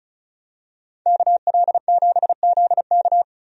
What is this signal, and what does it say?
Morse code, heard as a keyed tone, about 34 wpm, 700 Hz. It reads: KL7ZK
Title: USA Call Signs - 5 Characters - Repeated 3x - 35wpm